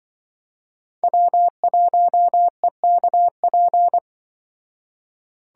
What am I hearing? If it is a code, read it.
Morse code, heard as a keyed tone, about 23 wpm, 700 Hz. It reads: W1EKP
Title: Speed Racing - USA 1x3 Call Signs 20wpm